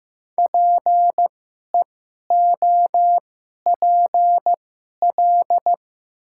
Morse code, heard as a keyed tone, about 15 wpm, spaced as for 11 wpm, 700 Hz. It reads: PEOPL
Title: 2x Character Spacing - Top 100 Words - 15wpm